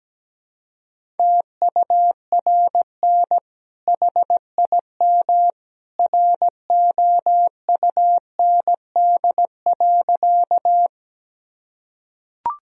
Morse code, heard as a keyed tone, about 17 wpm, 700 Hz. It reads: TURN HIM ROUND.
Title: Sentences from Top 1000 Words - Review Part 1 - 17wpm